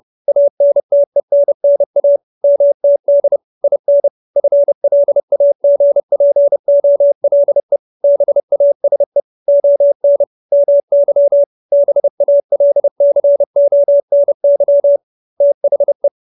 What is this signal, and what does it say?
Morse code, heard as a keyed tone, about 30 wpm, 570 Hz. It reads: ANTENNA MTD IN FLAGPOLE BASE ON MY BALCONY THE